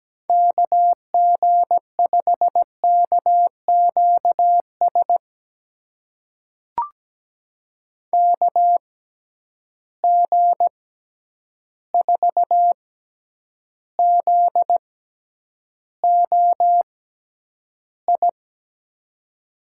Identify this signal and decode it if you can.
Morse code, heard as a keyed tone, about 17 wpm, spaced as for 9 wpm, 700 Hz. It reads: KG5KQS KG4ZOI
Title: USA Call Signs - 6 Characters - 6x Character Spacing - 17wpm